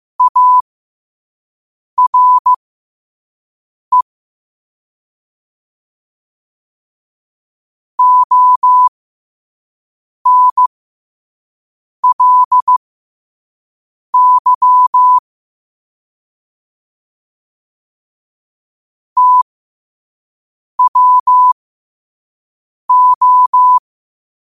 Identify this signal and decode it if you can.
Morse code, heard as a keyed tone, about 15 wpm, spaced as for 5 wpm, 1000 Hz. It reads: ARE ONLY TWO
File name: qotdfiles / 5wpm